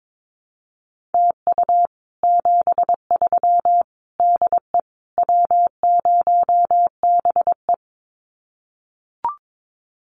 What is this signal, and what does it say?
Morse code, heard as a keyed tone, about 22 wpm, 700 Hz. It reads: TU 73 DE W0BE
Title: Common QSO Phrases Encore 22wpm